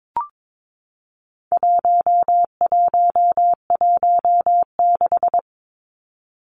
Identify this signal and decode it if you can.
Morse code, heard as a keyed tone, about 22 wpm, 700 Hz. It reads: 1116
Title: Numbers - 4-Digit 22wpm